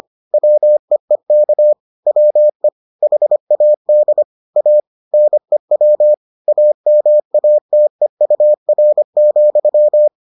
Morse code, heard as a keyed tone, about 25 wpm, 600 Hz. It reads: WEEK WE HAD A NEW AMATEUR,